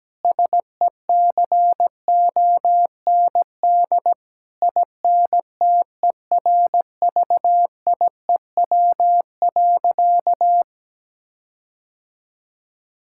Morse code, heard as a keyed tone, about 17 wpm, 700 Hz. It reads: SECOND INTERVIEW.
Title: Sentences from Top 1600 Words 17wpm